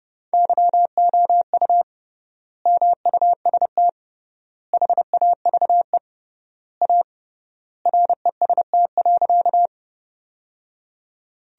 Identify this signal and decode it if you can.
Morse code, heard as a keyed tone, about 30 wpm, 700 Hz. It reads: YOU MUST HAVE A REST.
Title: 3x Word-Spacing - Sentences from Top 500 Words - 30wpm